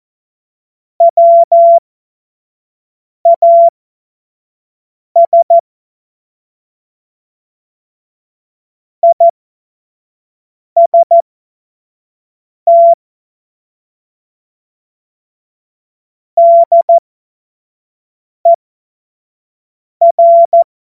Morse code, heard as a keyed tone, about 14 wpm, spaced as for 5 wpm, 680 Hz. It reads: WAS IST DER